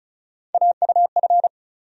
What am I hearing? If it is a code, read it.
Morse code, heard as a keyed tone, about 36 wpm, 700 Hz. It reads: AUF
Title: USA Call Sign Postfixes - 3 Letters 35wpm